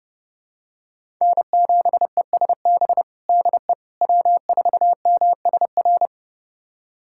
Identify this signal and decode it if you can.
Morse code, heard as a keyed tone, about 32 wpm, 700 Hz. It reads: N7ESB DE W4MSR
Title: USA Callsign DE Callsign - 5 Characters - 30wpm